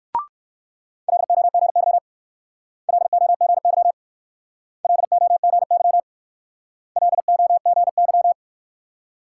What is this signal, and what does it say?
Morse code, heard as a keyed tone, about 60 wpm, 690 Hz. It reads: LOGY LOGY LOGY LOGY